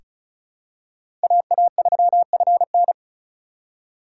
Morse code, heard as a keyed tone, about 34 wpm, 700 Hz. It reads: AA3FN
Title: USA Call Signs - 5 Characters - 3x Character Spacing - 35wpm